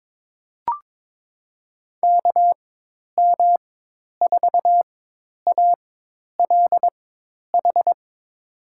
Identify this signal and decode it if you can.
Morse code, heard as a keyed tone, about 22 wpm, spaced as for 10 wpm, 700 Hz. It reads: KM4ALH
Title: USA Call Signs - 6 Characters - 4x Character Spacing - 22wpm